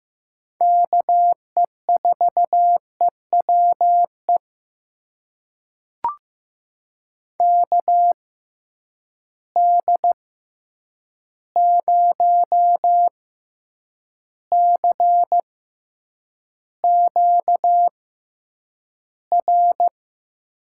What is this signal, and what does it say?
Morse code, heard as a keyed tone, about 15 wpm, spaced as for 8 wpm, 700 Hz. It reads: KE4EWE KD0CQR
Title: USA Call Signs - 6 Characters - 6x Character Spacing - 15wpm